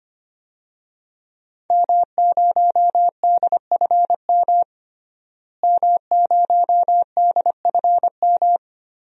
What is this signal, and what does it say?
Morse code, heard as a keyed tone, about 25 wpm, 700 Hz. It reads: M0DFM M0DFM
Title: DXCC-100 Call Signs - 5 Characters - Repeated 3x - 25wpm